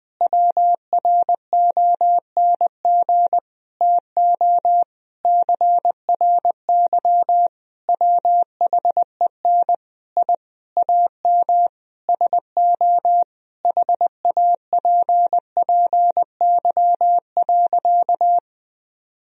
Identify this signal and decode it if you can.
Morse code, heard as a keyed tone, about 20 wpm, 700 Hz. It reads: WRONG TO CRY WHEN I AM SO HAPPY.
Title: Sentences from Top 1000 Words - Review Part 2 - 20wpm